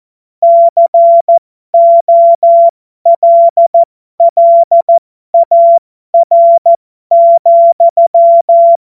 Morse code, heard as a keyed tone, about 14 wpm, spaced as for 12 wpm, 680 Hz. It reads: COLLAR,
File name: Morsekurs / Witze